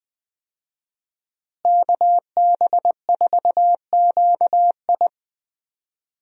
Morse code, heard as a keyed tone, about 20 wpm, 700 Hz. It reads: KB4QI
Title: USA Call Signs - 5 Characters - Repeated 3x - 20wpm